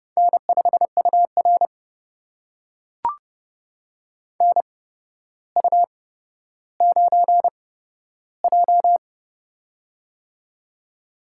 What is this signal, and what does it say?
Morse code, heard as a keyed tone, about 32 wpm, spaced as for 13 wpm, 700 Hz. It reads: N5UR NU9J
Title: USA Call Signs - 4 Characters - 8x Character Spacing - 30wpm